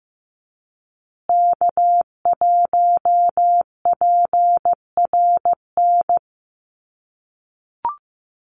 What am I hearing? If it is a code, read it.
Morse code, heard as a keyed tone, about 15 wpm, 700 Hz. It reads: K1PRN
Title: USA Call Signs Repeated 2x 15wpm